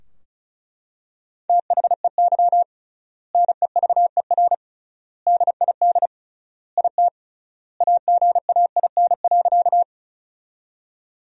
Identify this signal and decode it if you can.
Morse code, heard as a keyed tone, about 36 wpm, 700 Hz. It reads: THEY NEVER DID IT AGAIN.
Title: 3x Word-Spacing - Sentences from Top 300 Words - 35wpm